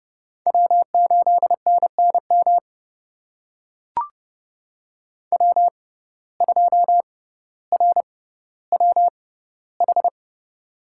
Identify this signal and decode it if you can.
Morse code, heard as a keyed tone, about 30 wpm, spaced as for 15 wpm, 700 Hz. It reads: W8NNM W2RWH
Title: USA Call Signs - 6x Character Spacing - 30wpm